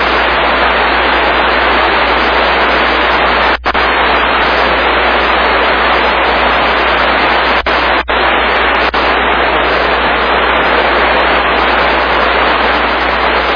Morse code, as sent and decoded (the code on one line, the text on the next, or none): none